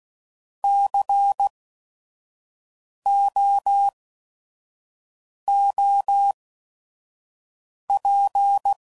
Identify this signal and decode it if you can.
Morse code, heard as a keyed tone, about 16 wpm, spaced as for 5 wpm, 800 Hz. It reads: COOP
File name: Prefixes / By First Letter / Cs